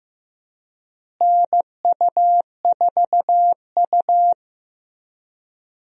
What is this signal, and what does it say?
Morse code, heard as a keyed tone, about 15 wpm, 700 Hz. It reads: NU4U
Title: USA Call Signs - 4 Characters - Repeated 3x - 15wpm